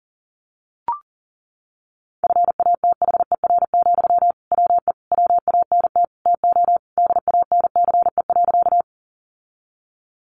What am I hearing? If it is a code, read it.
Morse code, heard as a keyed tone, about 40 wpm, 700 Hz. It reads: FATHER, WE WANT TO DANCE.
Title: Sentences from Top 700 Words 40wpm